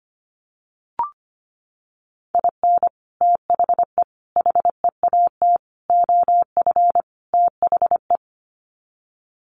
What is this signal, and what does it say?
Morse code, heard as a keyed tone, about 25 wpm, 700 Hz. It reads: IN THE HEAT OF THE
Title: Sets of 5 Words Encore2 25wpm